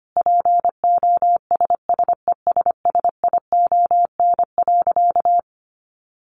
Morse code, heard as a keyed tone, about 25 wpm, 700 Hz. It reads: POSSESSION.
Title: Sentences from Top 1400 Words 25wpm